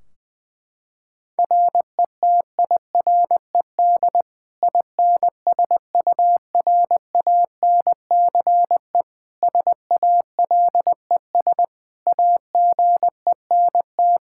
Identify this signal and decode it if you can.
Morse code, heard as a keyed tone, about 21 wpm, 700 Hz. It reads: RETIRED INSURANCE SALES AGENT
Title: Practice QSO - Exchange 3 - More Info - 20wpm